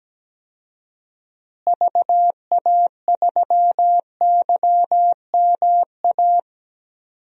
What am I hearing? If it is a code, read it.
Morse code, heard as a keyed tone, about 17 wpm, 700 Hz. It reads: VA3YMA
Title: DXCC-100 Call Signs - 6 Characters - Repeated 2x - 17wpm